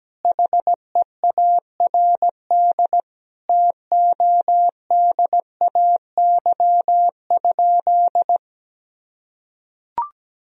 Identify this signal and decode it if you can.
Morse code, heard as a keyed tone, about 17 wpm, 700 Hz. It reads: HEARD TODAY?
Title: Sentences from Top 1000 Words - Review Part 4 - 17wpm